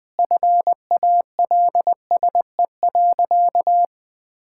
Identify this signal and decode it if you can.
Morse code, heard as a keyed tone, about 20 wpm, 700 Hz. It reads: FALSE.
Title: Sentences from Top 1600 Words 20wpm